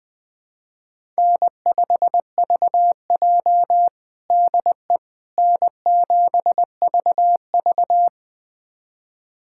Morse code, heard as a keyed tone, about 20 wpm, 700 Hz. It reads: N5VJ DE N7VV
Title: USA Callsign DE Callsign - 4 Characters - 20wpm